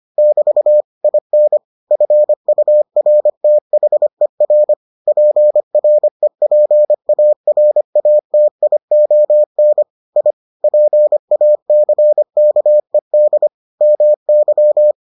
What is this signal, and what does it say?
Morse code, heard as a keyed tone, about 25 wpm, 600 Hz. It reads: <BT> IN FURTHER PREPARATION I PACKED MY